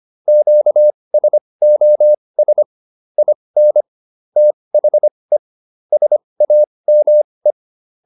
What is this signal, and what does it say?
Morse code, heard as a keyed tone, about 25 wpm, spaced as for 20 wpm, 600 Hz. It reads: QSOS IN THE SAME